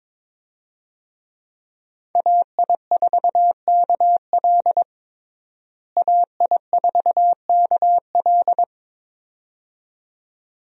Morse code, heard as a keyed tone, about 22 wpm, 700 Hz. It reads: AI4KL AI4KL
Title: USA Call Signs - 5 Characters - Repeated 2x - 22wpm